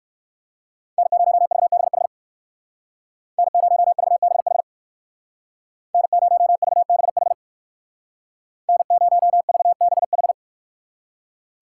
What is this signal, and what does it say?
Morse code, heard as a keyed tone, about 60 wpm, 690 Hz. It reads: N0VBH N0VBH N0VBH N0VBH